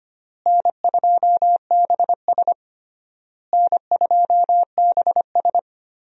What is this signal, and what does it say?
Morse code, heard as a keyed tone, about 25 wpm, 700 Hz. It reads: N2BS N2BS
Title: USA Call Signs - 4 Characters - Repeated 3x - 25wpm